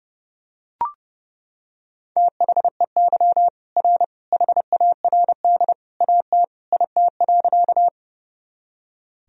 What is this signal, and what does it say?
Morse code, heard as a keyed tone, about 30 wpm, 700 Hz. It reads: THEY R HARD AT IT.
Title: Sentences from Top 300 Words and Abbreviations 30wpm